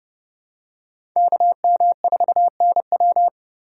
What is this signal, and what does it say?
Morse code, heard as a keyed tone, about 31 wpm, 700 Hz. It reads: KM4NW
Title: USA Call Signs - 5 Characters - 4x Character Spacing - 30wpm